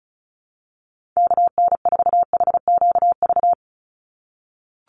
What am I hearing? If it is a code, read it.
Morse code, heard as a keyed tone, about 34 wpm, 700 Hz. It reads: KN4HQV
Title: USA Call Signs 35wpm